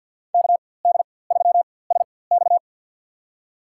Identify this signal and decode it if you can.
Morse code, heard as a keyed tone, about 52 wpm, spaced as for 23 wpm, 700 Hz. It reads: KD3SX
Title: USA Call Signs - 5 Characters - 4x Character Spacing - 50wpm